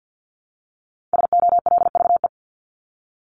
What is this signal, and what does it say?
Morse code, heard as a keyed tone, about 50 wpm, 700 Hz. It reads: SOLVE
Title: Top 751-1000 Words 50wpm